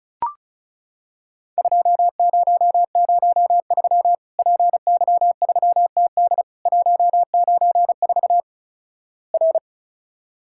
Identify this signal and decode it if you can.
Morse code, heard as a keyed tone, about 36 wpm, 700 Hz. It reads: 2003 PY3TD 194 R
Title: Worked All Europe DX Contest - QTC Exchange - 35wpm